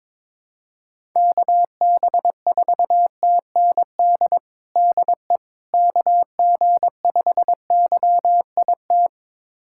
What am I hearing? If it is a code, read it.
Morse code, heard as a keyed tone, about 21 wpm, 700 Hz. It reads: KB4TND DE KG5YIT
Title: USA Callsign DE Callsign - 6 Characters - 22wpm